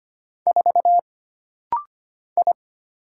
Morse code, heard as a keyed tone, about 25 wpm, 700 Hz. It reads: 4 I
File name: Single Character (Letter:Number:Punctuation:Prosign) - Rapid-Fire - 25wpm